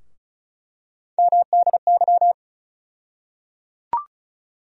Morse code, heard as a keyed tone, about 36 wpm, 700 Hz. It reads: MDY
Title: USA Call Sign Postfixes - 3 Letters 35wpm